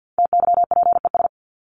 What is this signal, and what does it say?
Morse code, heard as a keyed tone, about 50 wpm, 700 Hz. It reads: TYPES
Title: ES:S Suffix - Top 100 Words with Base Words 50wpm